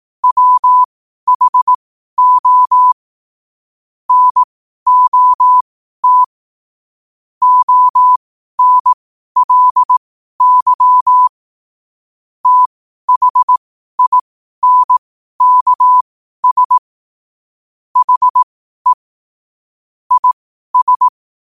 Morse code, heard as a keyed tone, about 18 wpm, spaced as for 12 wpm, 1000 Hz. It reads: WHO NOT ONLY THINKS HE IS
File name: qotdfiles / 12wpm